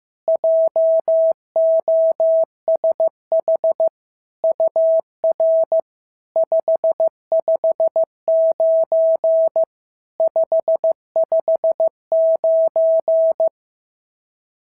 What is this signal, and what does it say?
Morse code, heard as a keyed tone, about 14 wpm, 650 Hz. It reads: JOSH UR 559 559
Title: SOTA QSO Practice 15wpm